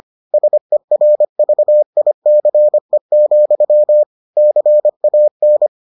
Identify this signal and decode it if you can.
Morse code, heard as a keyed tone, about 25 wpm, 600 Hz. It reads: SERVICE, CAN